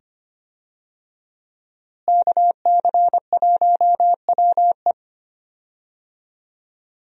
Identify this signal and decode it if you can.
Morse code, heard as a keyed tone, about 25 wpm, 700 Hz. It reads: KC1WE
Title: USA Call Signs - 5 Characters - Repeated 2x - 25wpm